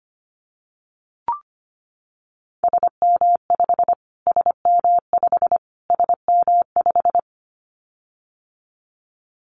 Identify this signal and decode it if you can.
Morse code, heard as a keyed tone, about 25 wpm, 700 Hz. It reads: SM5 SM5 SM5
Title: Callsign Prefixes for Top 25 DXCC - Repeated 3x - 25wpm